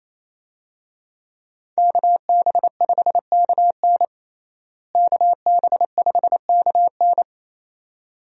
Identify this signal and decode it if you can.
Morse code, heard as a keyed tone, about 28 wpm, 700 Hz. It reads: KB5KN KB5KN
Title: USA Call Signs - 5 Characters - Repeated 2x - 28wpm